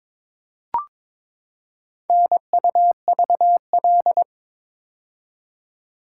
Morse code, heard as a keyed tone, about 22 wpm, 700 Hz. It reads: NUVL
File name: practice / 22wpm / Four Letter-Number Combinations - 22wpm